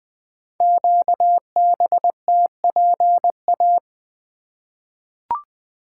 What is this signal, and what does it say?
Morse code, heard as a keyed tone, about 20 wpm, 700 Hz. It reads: QBTPA
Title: Code Groups - 5 Letters - 20wpm